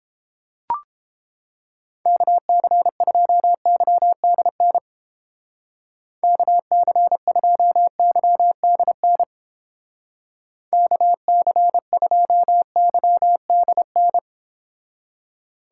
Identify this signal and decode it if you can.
Morse code, heard as a keyed tone, about 30 wpm, 700 Hz. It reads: KC2YDN KC2YDN KC2YDN